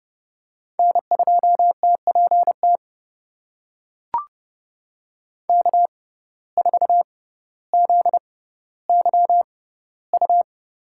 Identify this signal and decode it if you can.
Morse code, heard as a keyed tone, about 29 wpm, spaced as for 15 wpm, 700 Hz. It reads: N2TPT K4ZYU
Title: USA Call Signs - 5 Characters - 6x Character Spacing - 30wpm